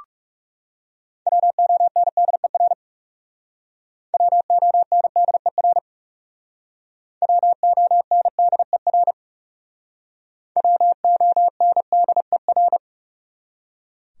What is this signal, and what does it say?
Morse code, heard as a keyed tone, about 40 wpm, 700 Hz. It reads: WONDER WONDER WONDER WONDER